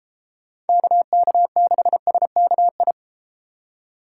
Morse code, heard as a keyed tone, about 30 wpm, 700 Hz. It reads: KK6SKI